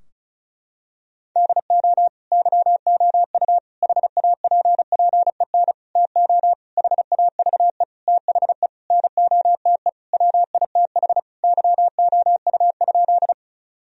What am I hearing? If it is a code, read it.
Morse code, heard as a keyed tone, about 34 wpm, 700 Hz. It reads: DO YOU HAPPEN TO HAVE THE NOTE WITH YOU?
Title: Sentences from Top 1000 Words - Review Part 2 - 35wpm